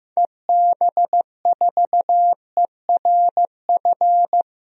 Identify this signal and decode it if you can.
Morse code, heard as a keyed tone, about 15 wpm, 700 Hz. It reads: EB4ERF
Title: DXCC-100 Call Signs - 6 Characters - Repeated 2x - 15wpm